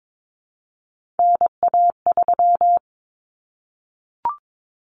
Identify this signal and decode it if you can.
Morse code, heard as a keyed tone, about 22 wpm, 700 Hz. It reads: NA3
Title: Common Cut Numbers - 3-Digit 22wpm